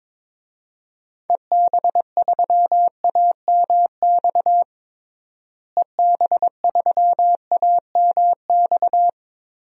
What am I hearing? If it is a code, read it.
Morse code, heard as a keyed tone, about 21 wpm, 700 Hz. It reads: EB3AMX EB3AMX
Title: DXCC-100 Call Signs - 6 Characters - Repeated 3x - 22wpm